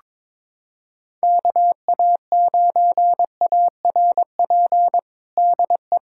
Morse code, heard as a keyed tone, about 22 wpm, 700 Hz. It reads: KA9ARP DE
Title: USA Callsign DE Callsign - 6 Characters - 22wpm